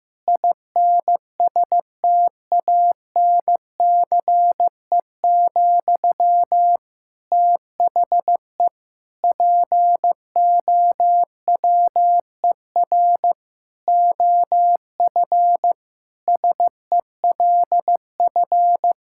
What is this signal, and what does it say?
Morse code, heard as a keyed tone, about 14 wpm, 700 Hz. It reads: INSTANCE, THE POWER OF SELF
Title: Sentences from Top 1000 Words - Review Part 5 - 15wpm